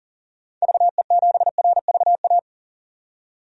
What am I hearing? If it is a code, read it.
Morse code, heard as a keyed tone, about 38 wpm, 700 Hz. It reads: VE7RVA
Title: DXCC-100 Call Signs - 6 Characters - 40wpm